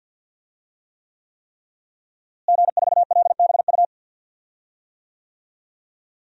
Morse code, heard as a keyed tone, about 51 wpm, 700 Hz. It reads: G4LBU
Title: DXCC-100 Call Signs - 5 Characters - Repeated 3x - 50wpm